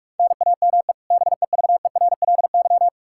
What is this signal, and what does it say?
Morse code, heard as a keyed tone, about 46 wpm, 700 Hz. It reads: NAME BEVERLY
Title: Practice QSO - Exchange 1 - Beginning - 45wpm